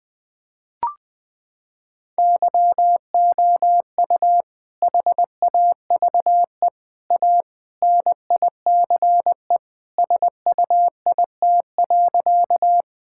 Morse code, heard as a keyed tone, about 21 wpm, 700 Hz. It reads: YOU HAVE A NICE SUIT.
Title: Sentences from Top 1600 Words 20wpm